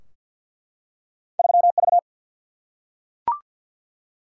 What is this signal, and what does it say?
Morse code, heard as a keyed tone, about 51 wpm, 700 Hz. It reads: 3 V
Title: Single-Letter Single-Number 50wpm